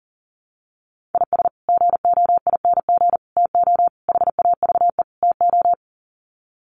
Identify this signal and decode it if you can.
Morse code, heard as a keyed tone, about 40 wpm, 700 Hz. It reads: IS GOING TO HAVE TO
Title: Sets of 5 Words 40wpm